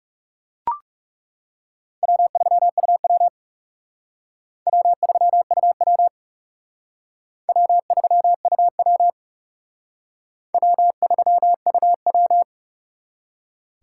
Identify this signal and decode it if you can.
Morse code, heard as a keyed tone, about 37 wpm, 700 Hz. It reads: W3UW W3UW W3UW W3UW
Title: Speed Racing - USA 1x2 Call Signs 30wpm